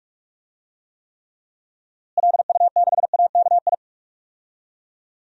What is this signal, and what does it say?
Morse code, heard as a keyed tone, about 46 wpm, 700 Hz. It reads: LU6AKI
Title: DXCC-100 Call Signs - 6 Characters - Repeated 3x - 45wpm